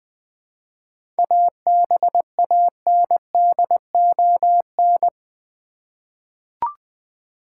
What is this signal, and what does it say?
Morse code, heard as a keyed tone, about 20 wpm, 700 Hz. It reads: ABANDON
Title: 7 Letter Words - First 3 Letters Given - 20wpm